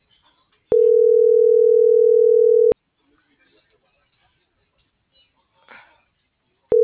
Ringback tone (North America), 440 + 480 Hz.